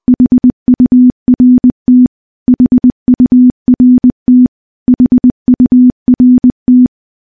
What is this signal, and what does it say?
Morse code, heard as a keyed tone, about 21 wpm, 250 Hz. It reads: HURT HURT HURT
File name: morse / morse-training-20WPM